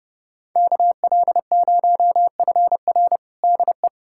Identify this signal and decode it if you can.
Morse code, heard as a keyed tone, about 30 wpm, 700 Hz. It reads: KL0FR DE
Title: USA Callsign DE Callsign - 5 Characters - 30wpm